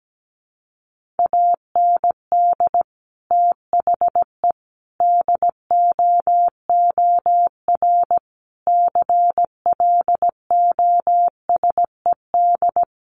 Morse code, heard as a keyed tone, about 17 wpm, 700 Hz. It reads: AND THE DOOR CLOSED
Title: Sentences from Top 1400 Words 17wpm